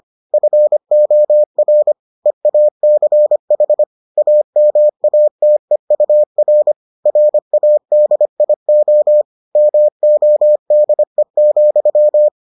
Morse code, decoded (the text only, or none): FOR EACH AMATEUR RADIO MODE,